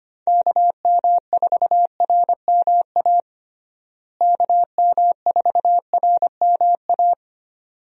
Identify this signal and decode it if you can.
Morse code, heard as a keyed tone, about 25 wpm, 700 Hz. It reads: KM4RMA KM4RMA
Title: USA Call Signs - 6 Characters - Repeated 2x - 25wpm